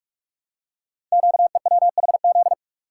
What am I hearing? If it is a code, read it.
Morse code, heard as a keyed tone, about 46 wpm, 700 Hz. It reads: QEWHZ
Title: Code Groups - 5 Letter:Numbers - 45wpm